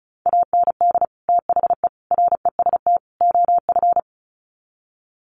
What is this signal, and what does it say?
Morse code, heard as a keyed tone, about 34 wpm, 700 Hz. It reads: AND THE REST OF
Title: Sets of 4 Words 35wpm